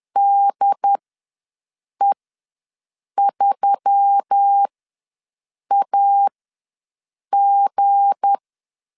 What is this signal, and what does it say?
Morse code, heard as a keyed tone, about 10 wpm, spaced as for 6 wpm, 800 Hz. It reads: DE3AG
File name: LEZIONI / 7 GF9Z / 7 G ws 10wpm 3843